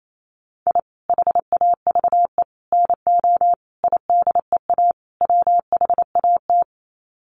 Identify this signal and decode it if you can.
Morse code, heard as a keyed tone, about 28 wpm, 700 Hz. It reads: I HAVE NO IDEA WHAT
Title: Sets of 5 Words Encore 28wpm